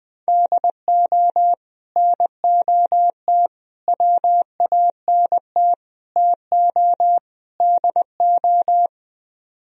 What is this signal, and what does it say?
Morse code, heard as a keyed tone, about 20 wpm, 700 Hz. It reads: DO NOT WANT TO DO